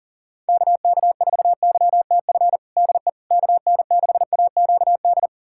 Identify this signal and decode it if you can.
Morse code, heard as a keyed tone, about 40 wpm, 700 Hz. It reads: KK4YTF DE KN6AQD
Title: USA Callsign DE Callsign - 6 Characters - 40wpm